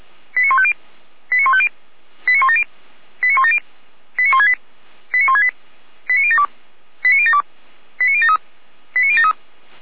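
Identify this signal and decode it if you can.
Five-tone selective call (ZVEI), heard as quick runs of five tones, tones of about 70 ms. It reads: 89150 89150 89170 89170 89168 89168 89072 89072 89073 89073
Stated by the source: Received on an SDR as a ZVEI five-tone call, code 89169.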